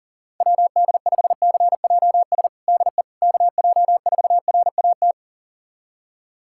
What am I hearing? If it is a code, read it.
Morse code, heard as a keyed tone, about 40 wpm, 700 Hz. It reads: WD5CJS DE KJ4RAT
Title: USA Callsign DE Callsign 40wpm